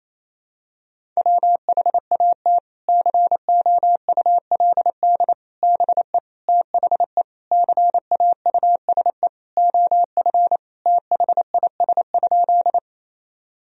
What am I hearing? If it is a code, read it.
Morse code, heard as a keyed tone, about 28 wpm, 700 Hz. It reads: WHAT COULD BE THE CAUSE OF THIS?
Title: Sentences from Top 1000 Words - Review Part 1 - 28wpm